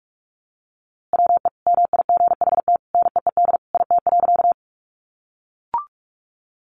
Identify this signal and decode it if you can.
Morse code, heard as a keyed tone, about 44 wpm, 700 Hz. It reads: WE MIGHT NEED IT.
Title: A Sound Of Thunder 45wpm